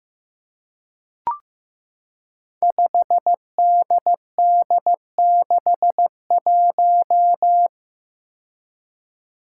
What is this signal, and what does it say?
Morse code, heard as a keyed tone, about 15 wpm, 700 Hz. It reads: 5DD61
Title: Five Challenging Letter-Number Combinations 15wpm